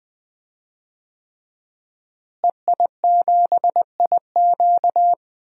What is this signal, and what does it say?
Morse code, heard as a keyed tone, about 20 wpm, 700 Hz. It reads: EI7IQ
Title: DXCC-100 Call Signs - 5 Characters - Repeated 3x - 20wpm